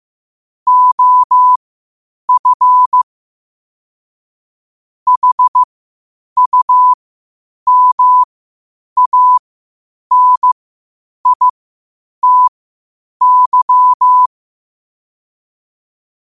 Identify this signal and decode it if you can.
Morse code, heard as a keyed tone, about 15 wpm, spaced as for 8 wpm, 1000 Hz. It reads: OF HUMANITY
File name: qotdfiles / 8wpm